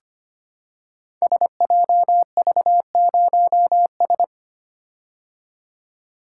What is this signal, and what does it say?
Morse code, heard as a keyed tone, about 25 wpm, 700 Hz. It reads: SJV0S
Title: Code Groups - 5 Letter:Numbers - 25wpm